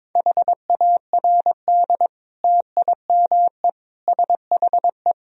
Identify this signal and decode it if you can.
Morse code, heard as a keyed tone, about 22 wpm, 700 Hz. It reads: HARD TIME SHE
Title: Sentences from Top 1000 Words - Review Part 1 - 22wpm